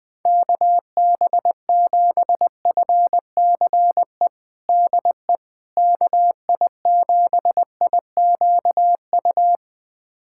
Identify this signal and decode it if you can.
Morse code, heard as a keyed tone, about 20 wpm, 700 Hz. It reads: KB7FCE DE KI7IQU
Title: USA Callsign DE Callsign - 6 Characters - 20wpm